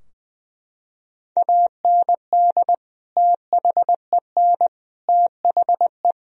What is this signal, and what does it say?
Morse code, heard as a keyed tone, about 20 wpm, 700 Hz. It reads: AND THEN THE
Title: Sentences from Top 1000 Words - Review Part 3 - 20wpm